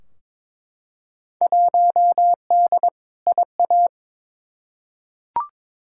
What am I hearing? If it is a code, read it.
Morse code, heard as a keyed tone, about 22 wpm, 700 Hz. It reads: 1D IA
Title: Field Day Exchanges - 22wpm